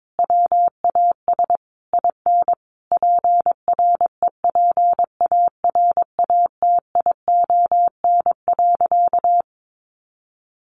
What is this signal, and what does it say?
Morse code, heard as a keyed tone, about 22 wpm, 700 Hz. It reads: WAS IN PREPARATION.
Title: Sentences from Top 1400 Words 22wpm